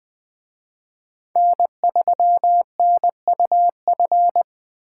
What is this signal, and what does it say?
Morse code, heard as a keyed tone, about 20 wpm, 700 Hz. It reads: N3NUF
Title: Speed Racing - USA 1x3 Call Signs 17wpm